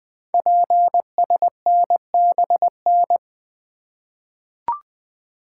Morse code, heard as a keyed tone, about 20 wpm, 700 Hz. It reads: PSNBN
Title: Code Groups - 5 Letters - 20wpm